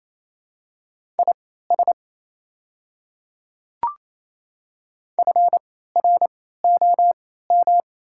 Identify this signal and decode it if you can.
Morse code, heard as a keyed tone, about 27 wpm, spaced as for 16 wpm, 700 Hz. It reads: IS FROM
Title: 3x Character Spacing - Top 100 Words - 28wpm